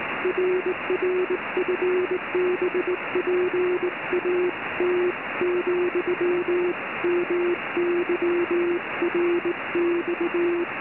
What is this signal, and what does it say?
Morse code, heard as a keyed tone, about 19 wpm, 380 Hz, with noise. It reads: RR FB PAT, MY RX